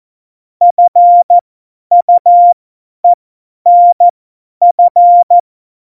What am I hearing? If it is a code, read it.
Morse code, heard as a keyed tone, about 14 wpm, spaced as for 10 wpm, 700 Hz. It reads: FUENF
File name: Morsekurs / Witze / Witz 23. April 2019, 50 F70